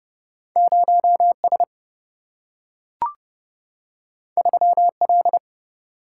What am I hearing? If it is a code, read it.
Morse code, heard as a keyed tone, about 30 wpm, 700 Hz. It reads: 0 S 3 L
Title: Single-Letter Single-Number 30wpm